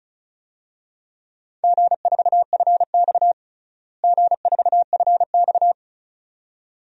Morse code, heard as a keyed tone, about 36 wpm, 700 Hz. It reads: G4FX G4FX
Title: DXCC-100 Call Signs - 4 Characters - Repeated 2x - 35wpm